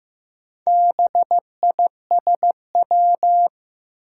Morse code, heard as a keyed tone, about 15 wpm, 700 Hz. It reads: BISW
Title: Four Challenging Letter-Number Combinations 15wpm